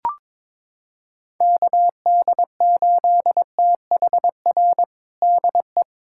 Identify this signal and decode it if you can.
Morse code, heard as a keyed tone, about 22 wpm, 700 Hz. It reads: KD8THR DE
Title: USA Callsign DE Callsign - 6 Characters - 22wpm